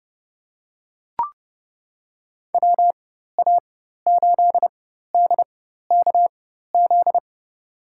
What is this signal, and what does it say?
Morse code, heard as a keyed tone, about 30 wpm, spaced as for 14 wpm, 700 Hz. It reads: WA8DKZ
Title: USA Call Signs - 6 Characters - 4x Character Spacing - 30wpm